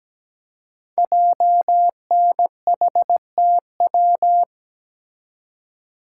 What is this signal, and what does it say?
Morse code, heard as a keyed tone, about 17 wpm, 700 Hz. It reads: JNHTW
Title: Code Groups - 5 Letters - 17wpm